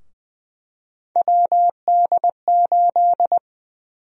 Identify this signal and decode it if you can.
Morse code, heard as a keyed tone, about 20 wpm, 700 Hz. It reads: WD8
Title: USA Call Sign Prefixes - Repeated 2x - 20wpm